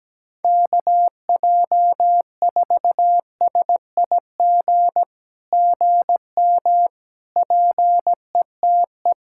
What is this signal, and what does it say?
Morse code, heard as a keyed tone, about 17 wpm, 700 Hz. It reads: KJ4SIG GM PETE
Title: SOTA QSO Practice 17wpm